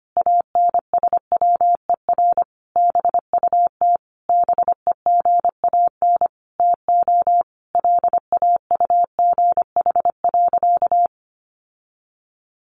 Morse code, heard as a keyed tone, about 25 wpm, 700 Hz. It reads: ANSWER BUT BEGAN TO LAUGH.
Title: Sentences from Top 600 Words 25wpm